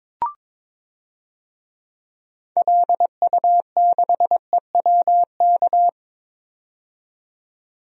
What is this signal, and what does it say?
Morse code, heard as a keyed tone, about 22 wpm, 700 Hz. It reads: LU6EWK
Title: DXCC-100 Call Signs - 6 Characters - 22wpm